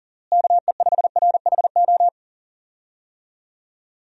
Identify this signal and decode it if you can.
Morse code, heard as a keyed tone, about 37 wpm, 700 Hz. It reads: KE5RHO
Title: USA Call Signs - 6 Characters - Repeated 3x - 40wpm